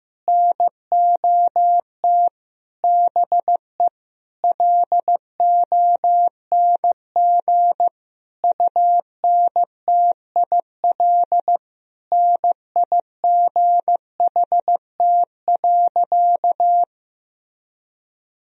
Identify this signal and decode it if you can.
Morse code, heard as a keyed tone, about 15 wpm, 700 Hz. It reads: NOT BE LONG UNTIL NIGHT.
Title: Sentences from Top 1000 Words - Review Part 1 - 15wpm